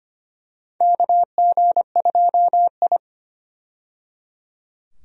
Morse code, heard as a keyed tone, about 25 wpm, 700 Hz. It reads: KG2I
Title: USA Call Signs - 4 Characters - 25wpm